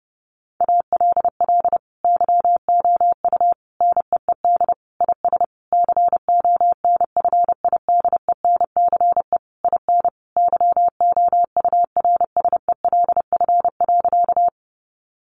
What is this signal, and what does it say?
Morse code, heard as a keyed tone, about 28 wpm, 700 Hz. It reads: ALL YOU NEED IS CONFIDENCE IN YOURSELF.
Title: Sentences from Top 1500 Words 30wpm